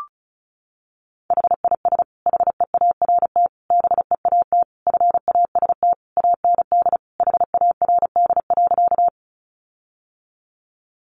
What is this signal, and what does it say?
Morse code, heard as a keyed tone, about 34 wpm, 700 Hz. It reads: HIS HEART BEAT FAST AND HARD.
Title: Sentences from Top 1300 Words 35wpm